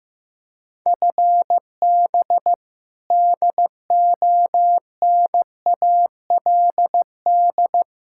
Morse code, heard as a keyed tone, about 16 wpm, 700 Hz. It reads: FB DONALD
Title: Practice QSO - Exchange 3 - More Info - 15wpm